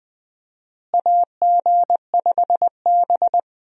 Morse code, heard as a keyed tone, about 20 wpm, 700 Hz. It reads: AG5B
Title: USA Call Signs - 4 Characters - Repeated 2x - 20wpm